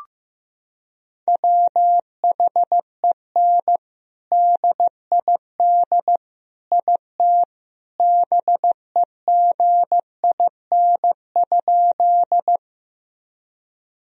Morse code, heard as a keyed tone, about 15 wpm, 700 Hz. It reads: WHEN DID IT BEGIN?
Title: Sentences from Top 1000 Words - Review Part 1 - 15wpm